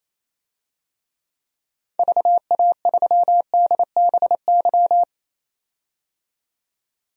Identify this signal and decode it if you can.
Morse code, heard as a keyed tone, about 27 wpm, 700 Hz. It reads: VA3DBY
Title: DXCC-100 Call Signs - 6 Characters - 28wpm